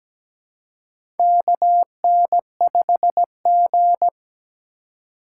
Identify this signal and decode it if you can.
Morse code, heard as a keyed tone, about 17 wpm, 700 Hz. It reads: KN5G
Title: USA Call Signs - 4 Characters - 4x Character Spacing - 17wpm